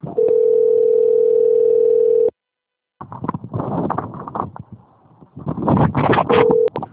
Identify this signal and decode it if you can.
Ringback tone (North America), 440 + 480 Hz.